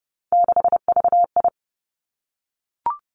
Morse code, heard as a keyed tone, about 30 wpm, 700 Hz. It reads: 6VI
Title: Three Challenging Letter-Number Combinations 30wpm